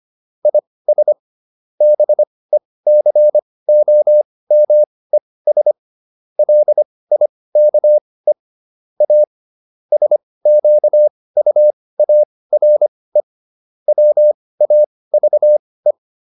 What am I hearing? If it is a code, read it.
Morse code, heard as a keyed tone, about 25 wpm, spaced as for 18 wpm, 600 Hz. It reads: IS BECOMES LIKE A SQUARE WAVE